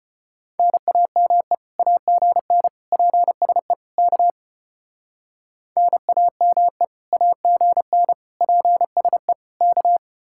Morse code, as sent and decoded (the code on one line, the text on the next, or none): -. .- -- . / .- --. -. / .--. ... / -.- / -. .- -- . / .- --. -. / .--. ... / -.-
NAME AGN PSE K NAME AGN PSE K